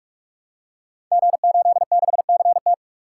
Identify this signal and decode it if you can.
Morse code, heard as a keyed tone, about 44 wpm, 700 Hz. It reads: G86CT
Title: Code Groups - 5 Letter:Numbers - 45wpm